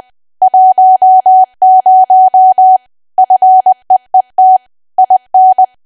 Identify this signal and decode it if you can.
Morse code, heard as a keyed tone, about 20 wpm, 750 Hz. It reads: 10 FEET IN
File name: cwa / W1AW-Level3 / 20WPM